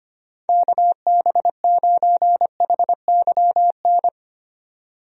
Morse code, heard as a keyed tone, about 25 wpm, 700 Hz. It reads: KB9HYN
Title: USA Call Signs - 6 Characters - 25wpm